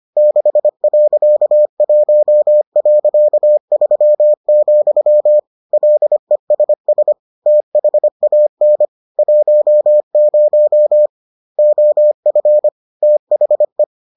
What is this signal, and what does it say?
Morse code, heard as a keyed tone, about 25 wpm, 600 Hz. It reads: 6.1.3, LESS THAN 10 OF THE